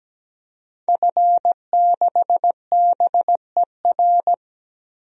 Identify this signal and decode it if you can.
Morse code, heard as a keyed tone, about 17 wpm, 700 Hz. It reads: F6BER
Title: DXCC-100 Call Signs - 5 Characters - 17wpm